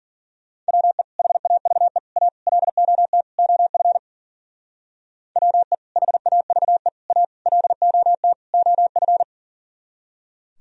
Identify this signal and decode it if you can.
Morse code, heard as a keyed tone, about 49 wpm, 700 Hz. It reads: WE HAVE A LOT OF WE HAVE A LOT OF
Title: Speed-Racing - Sets of 5 Words 40wpm